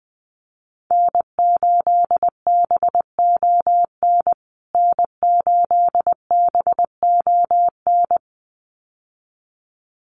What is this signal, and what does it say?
Morse code, heard as a keyed tone, about 20 wpm, 700 Hz. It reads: N8BON N8BON
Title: USA Call Signs Repeated 2x 20wpm